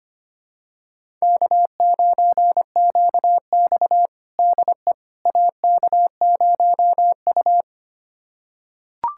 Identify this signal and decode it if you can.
Morse code, heard as a keyed tone, about 25 wpm, 700 Hz. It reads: K9QX DE AK0U
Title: USA Callsign DE Callsign - 4 Characters - 25wpm